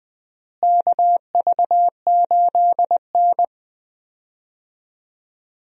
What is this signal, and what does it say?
Morse code, heard as a keyed tone, about 21 wpm, 700 Hz. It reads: KV8N
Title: USA Call Signs - 4 Characters - Repeated 3x - 20wpm